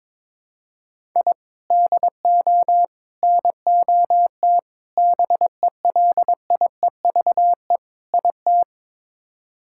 Morse code, heard as a keyed tone, about 22 wpm, 700 Hz. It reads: I DO NOT BELIEVE IT
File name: practice / 22wpm / Build A Phrase - 5-Word Phrases - 22wpm